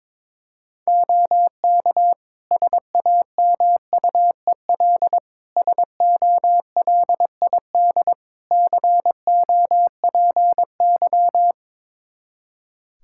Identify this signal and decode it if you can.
Morse code, heard as a keyed tone, about 22 wpm, 700 Hz. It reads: OK SAMUEL SOLID COPY
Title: K1USN SST QSO Practice 22wpm